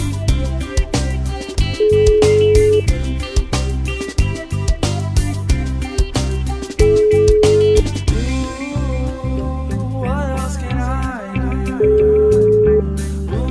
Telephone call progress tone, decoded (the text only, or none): ringback tone